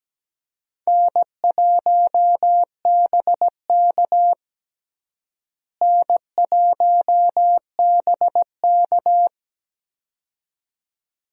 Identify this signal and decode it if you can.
Morse code, heard as a keyed tone, about 17 wpm, 700 Hz. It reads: N1BK N1BK
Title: USA Call Signs - 4 Characters - Repeated 2x - 17wpm